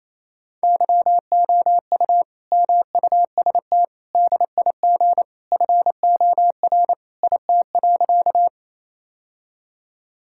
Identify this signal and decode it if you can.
Morse code, heard as a keyed tone, about 28 wpm, 700 Hz. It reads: YOU MUST DIG FOR IT.
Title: Sentences from Top 1600 Words 28wpm